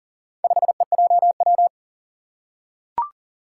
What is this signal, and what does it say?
Morse code, heard as a keyed tone, about 40 wpm, 700 Hz. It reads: 5EJW
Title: Four Challenging Letter-Number Combinations 40wpm